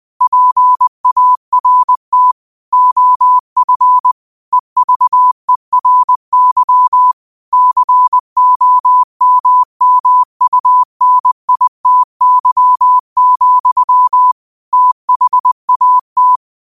Morse code, heard as a keyed tone, about 20 wpm, 1000 Hz. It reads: PART OF EVERY COMMUNITY, THAT
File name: qotdfiles / 20wpm